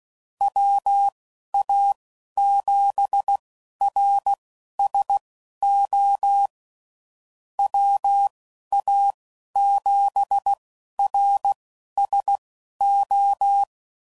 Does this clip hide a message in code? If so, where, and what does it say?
Yes: Morse code from 0.4 to 13.6 s, heard as a keyed tone, about 16 wpm, spaced as for 11 wpm, 800 Hz. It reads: WA7RSO WA7RSO